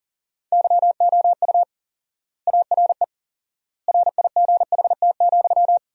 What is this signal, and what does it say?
Morse code, heard as a keyed tone, about 40 wpm, 700 Hz. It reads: YOU ARE RIGHT,
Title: 4x Word-Spacing - Sentences from Top 300 Words - 40wpm